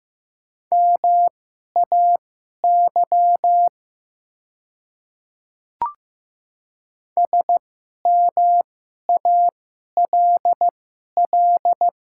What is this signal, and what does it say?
Morse code, heard as a keyed tone, about 15 wpm, spaced as for 11 wpm, 700 Hz. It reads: MAY SMALL
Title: 2x Character Spacing - Top 300 Words - 15wpm